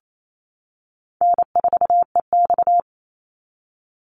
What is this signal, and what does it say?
Morse code, heard as a keyed tone, about 28 wpm, 700 Hz. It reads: N4EX
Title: NA SOTA Activator Callsigns 28wpm